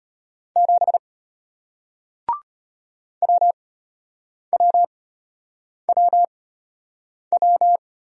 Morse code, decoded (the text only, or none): 7 WWWW